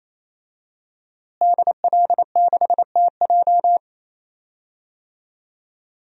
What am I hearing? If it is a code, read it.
Morse code, heard as a keyed tone, about 28 wpm, 700 Hz. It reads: DL6TJ
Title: DXCC-100 Call Signs - 5 Characters - Repeated 3x - 28wpm